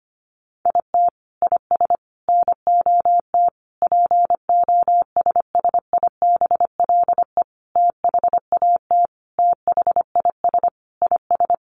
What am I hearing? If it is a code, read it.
Morse code, heard as a keyed tone, about 25 wpm, 700 Hz. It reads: IT IS NOT POSSIBLE THAT THIS IS
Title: Sentences from Top 1200 Words 25wpm